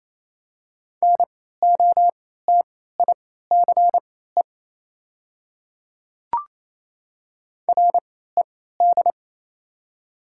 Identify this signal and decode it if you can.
Morse code, heard as a keyed tone, about 28 wpm, spaced as for 16 wpm, 700 Hz. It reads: NOTICE RED
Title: 3x Character Spacing - Top 500 Words - 28wpm